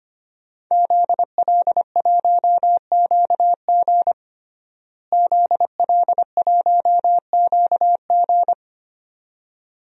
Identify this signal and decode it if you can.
Morse code, heard as a keyed tone, about 25 wpm, 700 Hz. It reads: ZL1QG ZL1QG
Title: DXCC-100 Call Signs - 5 Characters - Repeated 2x - 25wpm